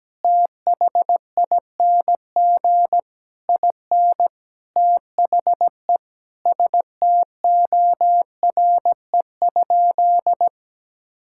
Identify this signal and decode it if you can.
Morse code, heard as a keyed tone, about 17 wpm, 700 Hz. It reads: THING IN THE STORE?
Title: Sentences from Top 1000 Words - Review Part 6 - 17wpm